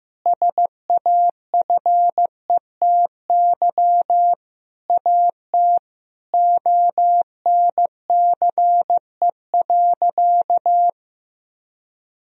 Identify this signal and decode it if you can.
Morse code, heard as a keyed tone, about 15 wpm, 700 Hz. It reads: SAFETY AT ONCE.
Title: Sentences from Top 1000 Words - Review Part 4 - 15wpm